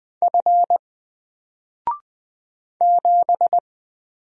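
Morse code, heard as a keyed tone, about 20 wpm, 700 Hz. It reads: F 7
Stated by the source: Single letter and number Morse drill at 20 wpm, from N4LCD.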